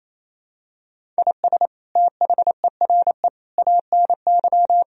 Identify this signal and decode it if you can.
Morse code, heard as a keyed tone, about 28 wpm, 700 Hz. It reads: IS THERE ANY
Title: Sentences from Top 1000 Words - Review Part 4 - 28wpm